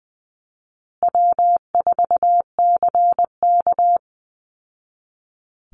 Morse code, heard as a keyed tone, about 20 wpm, 700 Hz. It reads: W4CK
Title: Easy Call Signs Repeated 2x - 20wpm